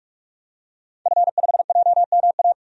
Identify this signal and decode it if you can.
Morse code, heard as a keyed tone, about 47 wpm, 700 Hz. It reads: F5JMA
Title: DXCC-100 Call Signs - 5 Characters - 45wpm